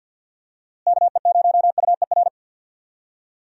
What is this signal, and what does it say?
Morse code, heard as a keyed tone, about 51 wpm, 700 Hz. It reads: KE0UER
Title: USA Call Signs - 6 Characters - Repeated 3x - 50wpm